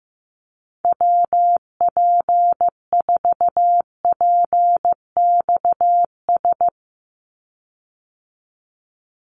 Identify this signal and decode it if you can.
Morse code, heard as a keyed tone, about 15 wpm, 700 Hz. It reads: WP4PXS
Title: USA Call Signs 15wpm